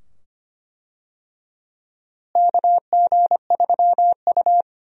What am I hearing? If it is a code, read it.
Morse code, heard as a keyed tone, about 25 wpm, 700 Hz. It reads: KG3U
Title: USA Call Signs - 4 Characters - Repeated 2x - 25wpm